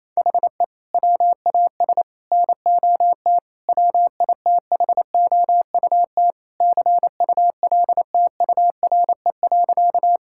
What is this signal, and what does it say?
Morse code, heard as a keyed tone, about 28 wpm, 700 Hz. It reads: HE WAS NOT WITHOUT CULTURE.
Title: Sentences from Top 1000 Words - Review Part 6 - 28wpm